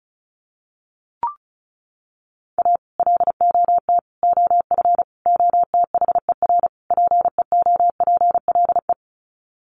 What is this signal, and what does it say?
Morse code, heard as a keyed tone, about 34 wpm, 700 Hz. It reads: A LOT OF OTHER PEOPLE
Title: Sets of 5 Words Encore2 35wpm